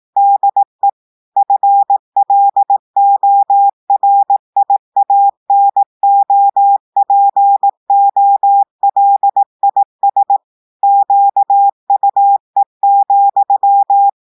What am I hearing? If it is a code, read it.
Morse code, heard as a keyed tone, about 18 wpm, 800 Hz. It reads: DE FLORIANOPOLIS QUE,
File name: audios / portugais / portugais texte en clair 1 vitesse 18 mots minutes